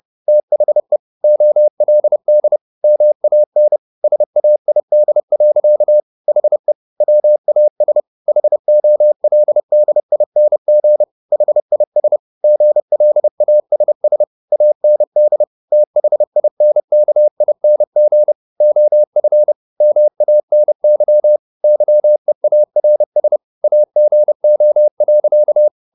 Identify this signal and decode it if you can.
Morse code, heard as a keyed tone, about 30 wpm, 600 Hz. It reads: THE OLD MAN SAID. HE WAS HOLDING HIS GLASS AND THINKING OF MANY YEARS AGO.
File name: the old man and the sea / the old man and the sea 1 30wpm